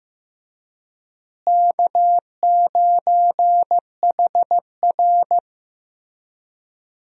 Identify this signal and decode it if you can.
Morse code, heard as a keyed tone, about 15 wpm, 700 Hz. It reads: K9HR
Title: USA Call Signs - 4 Characters - Repeated 3x - 15wpm